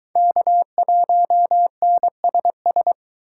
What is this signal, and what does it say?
Morse code, heard as a keyed tone, about 23 wpm, 700 Hz. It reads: K1NSS